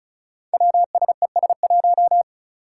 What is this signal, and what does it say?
Morse code, heard as a keyed tone, about 34 wpm, 700 Hz. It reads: WSES1
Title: Five Challenging Letter-Number Combinations 35wpm